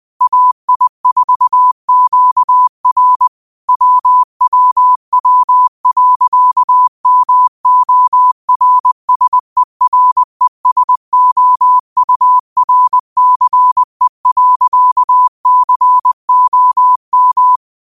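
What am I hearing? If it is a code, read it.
Morse code, heard as a keyed tone, about 20 wpm, 1000 Hz. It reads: AI4QR WWW.MORSERESOURCE.COM
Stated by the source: Morse quote practice at 20 wpm.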